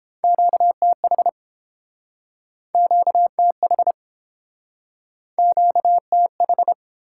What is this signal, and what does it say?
Morse code, heard as a keyed tone, about 30 wpm, 700 Hz. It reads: QTH QTH QTH